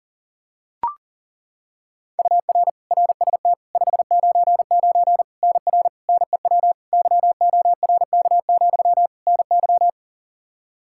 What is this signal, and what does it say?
Morse code, heard as a keyed tone, about 40 wpm, 700 Hz. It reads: UR RST 599 NR NEW YORK, NY
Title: K1USN SST QSO Practice 40wpm